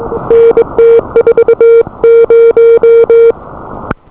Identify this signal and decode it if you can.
Morse code, heard as a keyed tone, about 22 wpm, 470 Hz. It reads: N T40